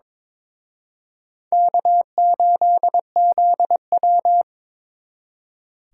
Morse code, heard as a keyed tone, about 26 wpm, 700 Hz. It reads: K8ZW
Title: Speed Racing - USA 1x2 Call Signs 22wpm